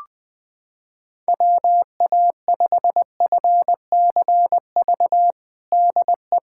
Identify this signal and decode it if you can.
Morse code, heard as a keyed tone, about 20 wpm, 700 Hz. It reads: WA5FCV DE
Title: USA Callsign DE Callsign - 6 Characters - 20wpm